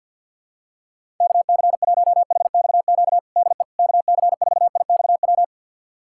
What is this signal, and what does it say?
Morse code, heard as a keyed tone, about 52 wpm, 700 Hz. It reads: KC1HXQ DE KC4IXW
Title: USA Callsign DE Callsign - 6 Characters - 50wpm